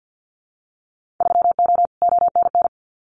Morse code, heard as a keyed tone, about 50 wpm, 700 Hz. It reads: 3O ONN
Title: Winter Field Day Exchanges 50wpm